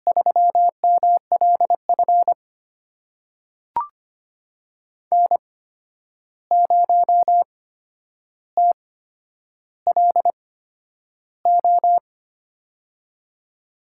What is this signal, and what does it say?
Morse code, heard as a keyed tone, about 25 wpm, spaced as for 10 wpm, 700 Hz. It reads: 3MLF N0TLO
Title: USA Call Signs - 5 Characters - 8x Character Spacing - 25wpm